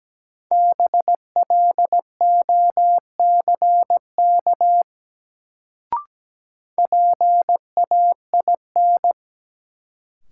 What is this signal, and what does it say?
Morse code, heard as a keyed tone, about 17 wpm, 700 Hz. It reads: BLOCK PAIN
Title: Sentences from Top 1600 Words - New Word Drill Alt - 17wpm